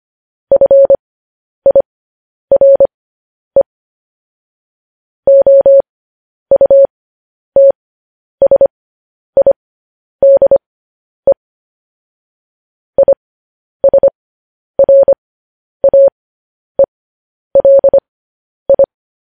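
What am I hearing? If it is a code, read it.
Morse code, heard as a keyed tone, about 25 wpm, spaced as for 10 wpm, 550 Hz. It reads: FIRE OUTSIDE ISRAELI
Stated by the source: News headlines sent in Morse code at 10 wpm.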